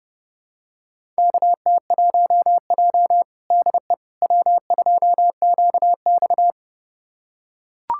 Morse code, heard as a keyed tone, about 30 wpm, 700 Hz. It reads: KT1J DE W2QX
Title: USA Callsign DE Callsign - 4 Characters - 30wpm